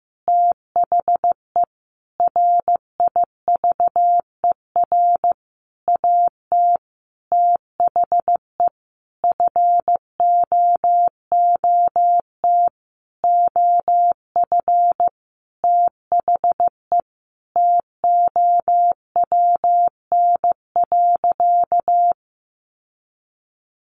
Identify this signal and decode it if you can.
Morse code, heard as a keyed tone, about 14 wpm, 700 Hz. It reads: THE RIVER AT THE FOOT OF THE TOWN.
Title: Sentences from Top 1400 Words 15wpm